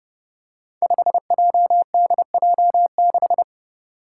Morse code, heard as a keyed tone, about 30 wpm, 700 Hz. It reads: 5JDJ6
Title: Five Challenging Letter-Number Combinations 30wpm